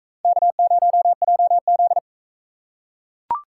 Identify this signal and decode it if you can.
Morse code, heard as a keyed tone, about 42 wpm, 700 Hz. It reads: K0JZ